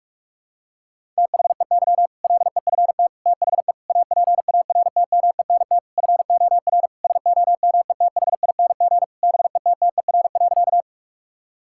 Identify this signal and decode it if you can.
Morse code, heard as a keyed tone, about 46 wpm, 700 Hz. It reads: THEY LEFT THE APARTMENT FOR SOMETHING BETTER.
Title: Sentences from Top 1000 Words - Review Part 6 - 45wpm